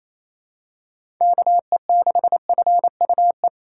Morse code, heard as a keyed tone, about 28 wpm, 700 Hz. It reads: KE6FUE
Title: USA Call Signs - 6 Characters - 2x Character Spacing - 28wpm